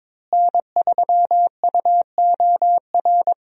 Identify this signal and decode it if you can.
Morse code, heard as a keyed tone, about 22 wpm, 700 Hz. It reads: N3UOR
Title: USA Call Signs - 5 Characters - Repeated 2x - 22wpm